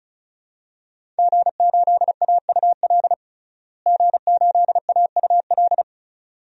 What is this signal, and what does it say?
Morse code, heard as a keyed tone, about 34 wpm, 700 Hz. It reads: G8AUL G8AUL
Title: DXCC-100 Call Signs - 5 Characters - Repeated 2x - 35wpm